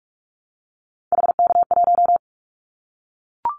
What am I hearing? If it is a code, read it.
Morse code, heard as a keyed tone, about 44 wpm, 700 Hz. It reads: HK1
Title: Callsign Prefixes for Top 100 DXCC - 45wpm